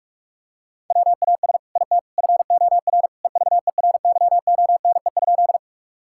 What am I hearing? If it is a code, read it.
Morse code, heard as a keyed tone, about 46 wpm, 700 Hz. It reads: WAS IT FOR EVERYONE?
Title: Sentences from Top 1000 Words - Review Part 5 - 45wpm